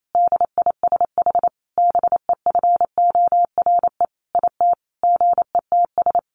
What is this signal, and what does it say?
Morse code, heard as a keyed tone, about 28 wpm, 700 Hz. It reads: DISH BEFORE IT GETS
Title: Sentences from Top 1300 Words 28wpm